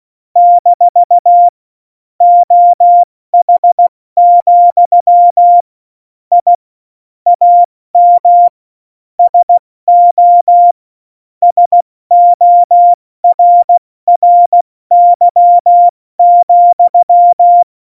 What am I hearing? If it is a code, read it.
Morse code, heard as a keyed tone, about 16 wpm, 700 Hz. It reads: - OH, I AM SO SORRY,
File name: Morsekurs / Witze / Witz 19 Mai, 2020 70 F80